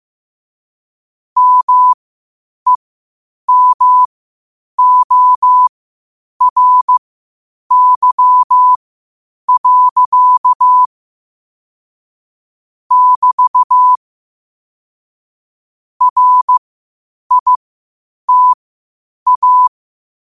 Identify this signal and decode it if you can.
Morse code, heard as a keyed tone, about 15 wpm, spaced as for 8 wpm, 1000 Hz. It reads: MEMORY. <BT> RITA
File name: qotdfiles / 8wpm